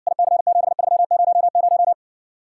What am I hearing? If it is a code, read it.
Morse code, heard as a keyed tone, about 60 wpm, 700 Hz. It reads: IC7300